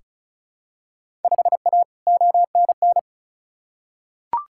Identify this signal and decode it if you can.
Morse code, heard as a keyed tone, about 36 wpm, 700 Hz. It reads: 5A ONN
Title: Field Day Exchanges Encore - 35wpm